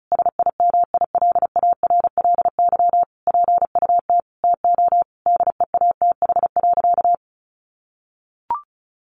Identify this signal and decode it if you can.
Morse code, heard as a keyed tone, about 34 wpm, 700 Hz. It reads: SIMILARLY PUT TO DEATH.
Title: Sentences from Top 1400 Words 35wpm